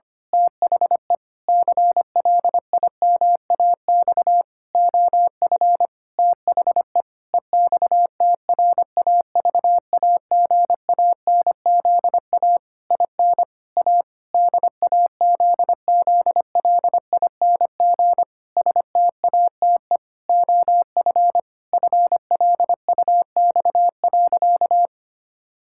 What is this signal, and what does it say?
Morse code, heard as a keyed tone, about 25 wpm, 700 Hz. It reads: THE CLIMAX OF THE EXTRAVAGANZA IN A DAZZLING STATE OF FLUX.
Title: Pangram Challenge v2 - 25wpm